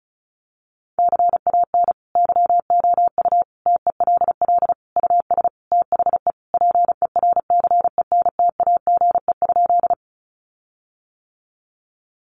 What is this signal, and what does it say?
Morse code, heard as a keyed tone, about 34 wpm, 700 Hz. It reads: CAN YOU TELL US THE PERCENTAGE?
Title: Sentences from Top 1300 Words 35wpm